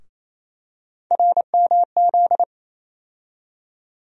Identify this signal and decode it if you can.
Morse code, heard as a keyed tone, about 28 wpm, 700 Hz. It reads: RMZ
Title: USA Call Sign Postfixes - 3 Letters 28wpm